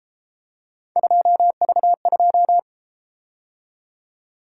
Morse code, heard as a keyed tone, about 31 wpm, 700 Hz. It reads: 2V2